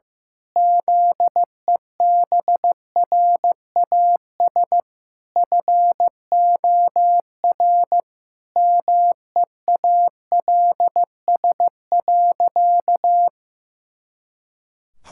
Morse code, heard as a keyed tone, about 15 wpm, 700 Hz. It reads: ZEBRAS FOR MEALS.
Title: Pangram Challenge v2 - 15wpm